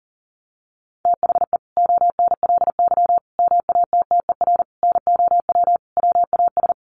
Morse code, heard as a keyed tone, about 40 wpm, 700 Hz. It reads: THE ONLY MATTER NOW WAS
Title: Sentences from Top 1400 Words 40wpm